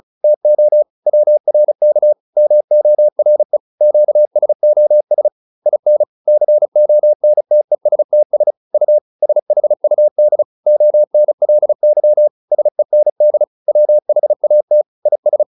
Morse code, heard as a keyed tone, about 34 wpm, 600 Hz. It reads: TO WRK MORE QSOS IN CONTESTS U SHUD ONLY SEND WHAT IS